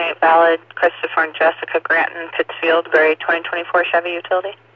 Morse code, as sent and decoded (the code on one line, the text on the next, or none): none